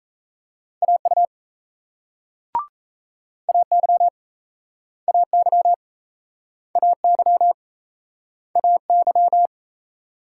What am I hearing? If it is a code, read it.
Morse code, heard as a keyed tone, about 38 wpm, 700 Hz. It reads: AU AY AY AY AY